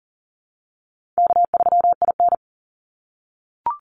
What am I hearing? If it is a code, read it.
Morse code, heard as a keyed tone, about 40 wpm, 700 Hz. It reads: K3IN
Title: Call Signs - 40wpm